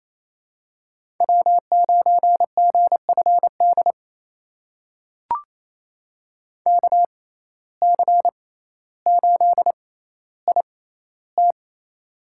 Morse code, heard as a keyed tone, about 28 wpm, spaced as for 14 wpm, 700 Hz. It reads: W9GFD KC8IT
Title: USA Call Signs - 5 Characters - 6x Character Spacing - 28wpm